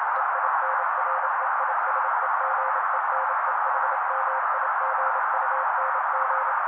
Morse code, heard as a keyed tone, about 28 wpm, 530 Hz. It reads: EINREISEGENEHMIGUNG
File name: daily / 27wpm noise